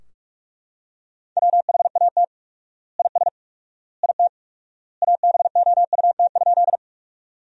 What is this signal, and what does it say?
Morse code, heard as a keyed tone, about 44 wpm, 700 Hz. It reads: WHAT IS IT ABOUT?
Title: 4x Word-Spacing - Sentences from Top 100 Words - 45wpm